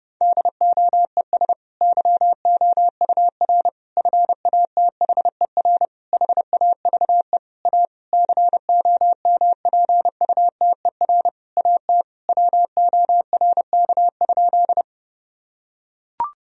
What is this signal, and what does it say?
Morse code, heard as a keyed tone, about 30 wpm, 700 Hz. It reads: DOES YOUR FATHER HAVE A COMPUTER AT WORK?
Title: Sentences from Top 1000 Words - Review Part 3 - 30wpm